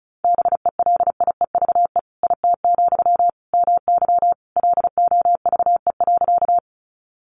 Morse code, heard as a keyed tone, about 34 wpm, 700 Hz. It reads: BELIEVE IT, MY LOVE.
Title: Sentences from Top 1000 Words 35wpm